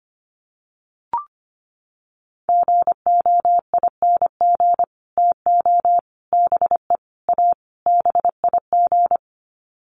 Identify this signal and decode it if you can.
Morse code, heard as a keyed tone, about 25 wpm, 700 Hz. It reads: GOING TO BE A BIG
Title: Sets of 5 Words Encore2 25wpm